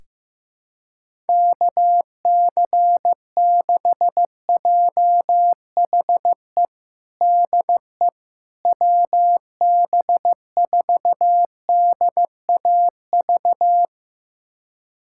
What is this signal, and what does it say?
Morse code, heard as a keyed tone, about 15 wpm, 700 Hz. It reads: KC6JHE DE WB4DAV
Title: USA Callsign DE Callsign 15wpm